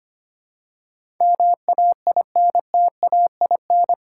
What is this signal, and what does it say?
Morse code, heard as a keyed tone, about 25 wpm, 700 Hz. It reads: MAINTAIN
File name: Eight Letter Words - 25wpm